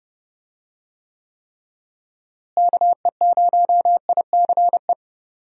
Morse code, heard as a keyed tone, about 30 wpm, 700 Hz. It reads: KE0ICE
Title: USA Call Signs - 6 Characters - Repeated 3x - 30wpm